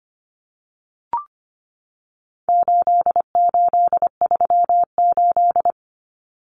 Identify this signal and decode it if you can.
Morse code, heard as a keyed tone, about 25 wpm, 700 Hz. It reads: 8838
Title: Numbers - 4-Digit 25wpm